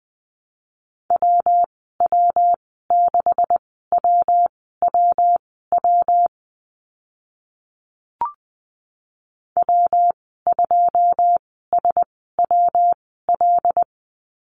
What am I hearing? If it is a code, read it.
Morse code, heard as a keyed tone, about 20 wpm, spaced as for 14 wpm, 700 Hz. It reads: WW6WWW W2SWL
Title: 140 Lesson 23g - Letter f - Call Signs 20wpm